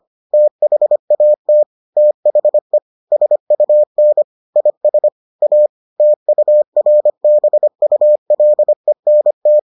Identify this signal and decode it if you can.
Morse code, heard as a keyed tone, about 25 wpm, 600 Hz. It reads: THAT THE SUN IS A TURBULENT